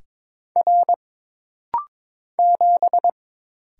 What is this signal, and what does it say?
Morse code, heard as a keyed tone, about 22 wpm, 700 Hz. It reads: R 7
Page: This rapid-fire timing is geared to push your ICR (Instant Character Recognition) skills.